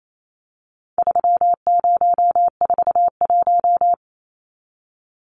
Morse code, heard as a keyed tone, about 28 wpm, 700 Hz. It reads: 3041
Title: Numbers - 4-Digit 28wpm